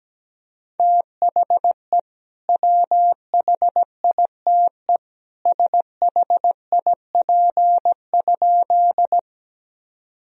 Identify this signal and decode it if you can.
Morse code, heard as a keyed tone, about 17 wpm, 700 Hz. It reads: THE WHITE SHIP?
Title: Sentences from Top 1000 Words - Review Part 2 - 17wpm